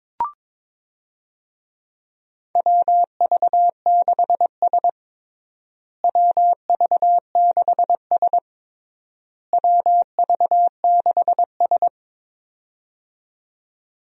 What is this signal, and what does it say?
Morse code, heard as a keyed tone, about 22 wpm, 700 Hz. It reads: WV6S WV6S WV6S
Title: USA Call Signs - 4 Characters - Repeated 3x - 22wpm